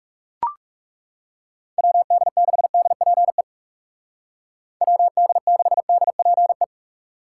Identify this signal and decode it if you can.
Morse code, heard as a keyed tone, about 40 wpm, 700 Hz. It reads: WD6DPE WD6DPE